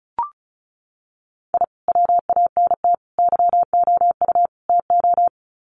Morse code, heard as a keyed tone, about 34 wpm, 700 Hz. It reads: I WANT YOU TO
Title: Sets of 4 Words 35wpm